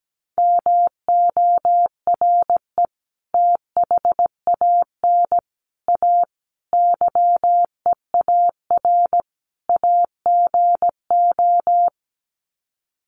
Morse code, heard as a keyed tone, about 17 wpm, 700 Hz. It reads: MORE THAN A YEAR AGO
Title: Sets of 5 Words Encore2 17wpm